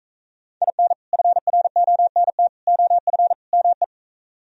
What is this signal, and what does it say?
Morse code, heard as a keyed tone, about 42 wpm, 700 Hz. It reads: IN FRONT OF ME